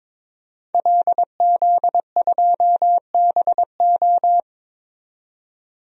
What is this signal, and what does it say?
Morse code, heard as a keyed tone, about 22 wpm, 700 Hz. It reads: LZ2BO
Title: DXCC-100 Call Signs - 5 Characters - 22wpm